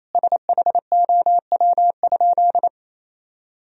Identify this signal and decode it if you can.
Morse code, heard as a keyed tone, about 28 wpm, 700 Hz. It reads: SHOW?
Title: Sentences from Top 1000 Words - Review Part 2 - 28wpm